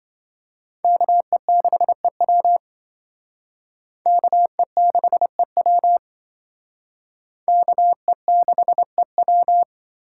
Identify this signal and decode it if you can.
Morse code, heard as a keyed tone, about 27 wpm, 700 Hz. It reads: KE6EW KE6EW KE6EW